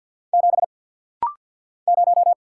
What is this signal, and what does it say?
Morse code, heard as a keyed tone, about 50 wpm, 700 Hz. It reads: 7 0
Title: Single Letter-Number - Rapid-Fire 50wpm